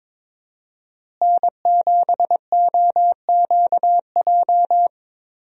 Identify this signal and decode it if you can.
Morse code, heard as a keyed tone, about 22 wpm, 690 Hz. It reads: N7OQJ
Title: USA Call Signs - 5 Characters - Repeated 3x - 22wpm